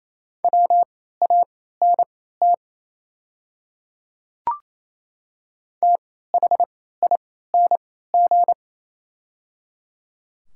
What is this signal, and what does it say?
Morse code, heard as a keyed tone, about 28 wpm, spaced as for 16 wpm, 700 Hz. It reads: WANT THING
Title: 3x Character Spacing - Top 200 Words - 28wpm